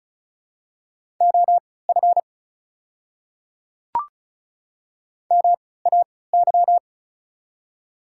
Morse code, heard as a keyed tone, about 36 wpm, spaced as for 20 wpm, 700 Hz. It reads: OF MAY
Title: 3x Character Spacing - Top 100 Words - 35wpm